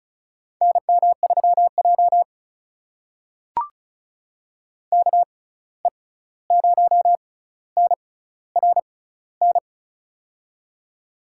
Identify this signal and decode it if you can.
Morse code, heard as a keyed tone, about 36 wpm, spaced as for 16 wpm, 700 Hz. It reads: NM3J KE0NRN
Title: USA Call Signs - 6x Character Spacing - 35wpm